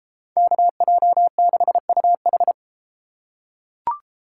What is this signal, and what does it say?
Morse code, heard as a keyed tone, about 31 wpm, 700 Hz. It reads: KJ6UH